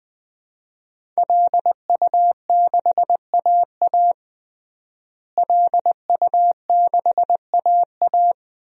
Morse code, heard as a keyed tone, about 20 wpm, 700 Hz. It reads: LU6AA LU6AA
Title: DXCC-100 Call Signs - 5 Characters - Repeated 3x - 20wpm